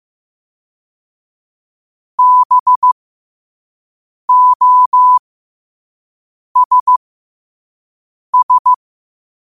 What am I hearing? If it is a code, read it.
Morse code, heard as a keyed tone, about 15 wpm, spaced as for 5 wpm, 1000 Hz. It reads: BOSS
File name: qotdfiles / 5wpm